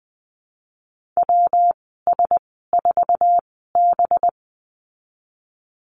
Morse code, heard as a keyed tone, about 20 wpm, spaced as for 15 wpm, 700 Hz. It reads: WS4B
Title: 186 Lesson 30g - Letter b - Call Signs 20wpm